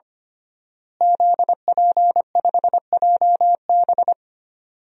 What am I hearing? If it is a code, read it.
Morse code, heard as a keyed tone, about 25 wpm, 700 Hz. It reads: ZP5JB
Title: DXCC-100 Call Signs - 5 Characters - Repeated 2x - 25wpm